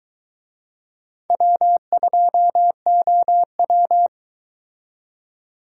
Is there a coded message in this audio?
Yes: Morse code from 1.3 to 4.1 s, heard as a keyed tone, about 22 wpm, 700 Hz. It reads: W2OW